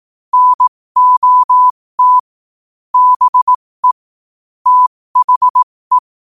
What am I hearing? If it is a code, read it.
Morse code, heard as a keyed tone, about 18 wpm, spaced as for 15 wpm, 1000 Hz. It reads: NOT BE THE
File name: qotdfiles / 15wpm